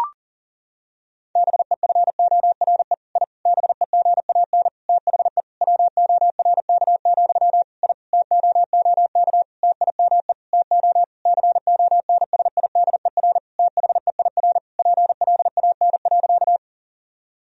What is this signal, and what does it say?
Morse code, heard as a keyed tone, about 40 wpm, 700 Hz. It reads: BEFORE I BEGAN THE WORK, I TOOK TIME TO CONSIDER THEIR PLAN.
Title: Sentences from Top 1000 Words - Review Part 4 - 40wpm